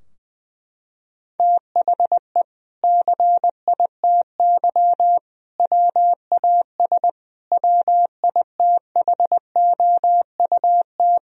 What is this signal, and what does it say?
Morse code, heard as a keyed tone, about 20 wpm, 700 Hz. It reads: THE CITY WAS WITHOUT
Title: Sentences from Top 1000 Words - Review Part 3 - 20wpm